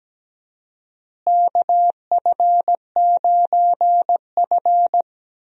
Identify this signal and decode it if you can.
Morse code, heard as a keyed tone, about 17 wpm, 700 Hz. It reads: KF9F
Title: USA Call Signs - 4 Characters - Repeated 2x - 17wpm